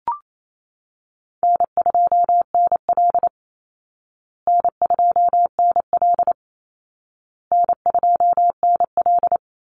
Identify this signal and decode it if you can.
Morse code, heard as a keyed tone, about 28 wpm, 700 Hz. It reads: N2NL N2NL N2NL
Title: Call Signs Repeated 2x - 28wpm